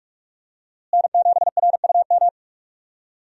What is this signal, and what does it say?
Morse code, heard as a keyed tone, about 46 wpm, 700 Hz. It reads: N7RUM
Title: USA Call Signs - 5 Characters - Repeated 2x - 45wpm